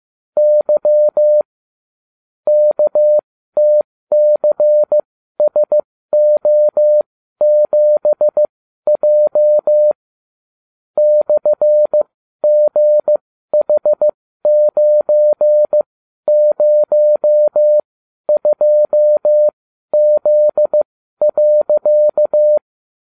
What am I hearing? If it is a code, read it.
Morse code, heard as a keyed tone, about 15 wpm, spaced as for 12 wpm, 600 Hz. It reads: Y KTCSO7J /GH902Z.